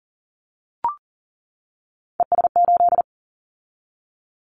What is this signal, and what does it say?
Morse code, heard as a keyed tone, about 38 wpm, 700 Hz. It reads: ES8
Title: Callsign Prefixes for Top 50 DXCC - 40wpm